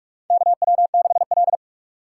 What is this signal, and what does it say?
Morse code, heard as a keyed tone, about 46 wpm, 700 Hz. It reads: KW6L
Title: USA Call Signs - 4 Characters - 6x Character Spacing - 45wpm